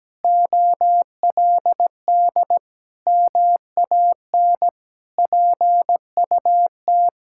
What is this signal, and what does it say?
Morse code, heard as a keyed tone, about 17 wpm, 700 Hz. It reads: OLD MAN PUT
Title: Sentences from Top 1600 Words 17wpm